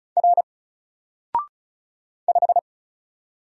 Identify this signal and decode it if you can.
Morse code, heard as a keyed tone, about 34 wpm, 700 Hz. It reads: R 5
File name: Single-Letter-Number-35wpm